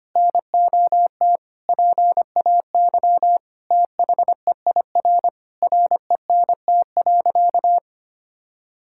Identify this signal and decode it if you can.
Morse code, heard as a keyed tone, about 25 wpm, 700 Hz. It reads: NOT PAY THEIR RENT.
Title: Sentences from Top 1000 Words - Review Part 5 - 25wpm